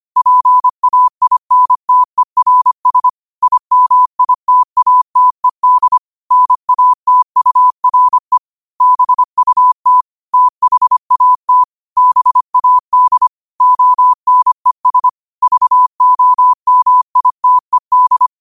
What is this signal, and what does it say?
Morse code, heard as a keyed tone, about 25 wpm, 1000 Hz. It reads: PAINTERS IMITATED NATURE BUT THAT BAD ONES VOMITED
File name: qotdfiles / 25wpm